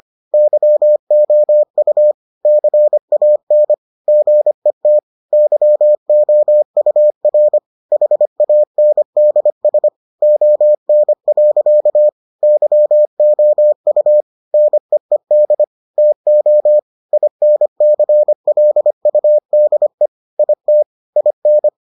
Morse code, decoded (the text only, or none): YOU CAN GET YOUR HANDS ON. YOU NEED TO INCLUDE IT IN